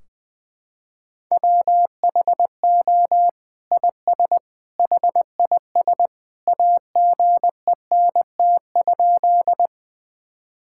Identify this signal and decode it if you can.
Morse code, heard as a keyed tone, about 21 wpm, 700 Hz. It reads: WHO IS HIS AGENT?
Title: Sentences from Top 1600 Words 20wpm